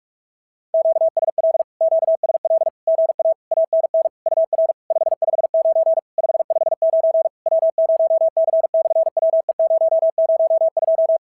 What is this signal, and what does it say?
Morse code, heard as a keyed tone, about 46 wpm, 690 Hz. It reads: QSL QSL GA ANN UR 559 559 W0C/WE002
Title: SOTA QSO Practice 45wpm